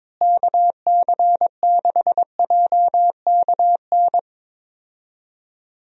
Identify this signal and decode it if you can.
Morse code, heard as a keyed tone, about 22 wpm, 700 Hz. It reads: KC6JKN
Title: USA Call Signs - 6 Characters - Repeated 2x - 22wpm